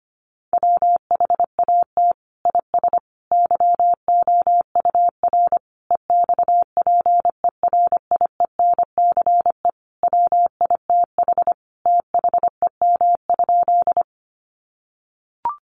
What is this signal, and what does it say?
Morse code, heard as a keyed tone, about 25 wpm, 700 Hz. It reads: WHAT IS YOUR EXPERIENCE WITH THEM?
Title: Sentences from Top 600 Words 25wpm